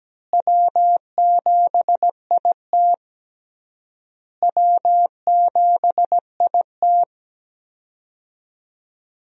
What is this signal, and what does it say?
Morse code, heard as a keyed tone, about 17 wpm, 700 Hz. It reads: W7IT W7IT
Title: USA Call Signs - 4 Characters - Repeated 3x - 17wpm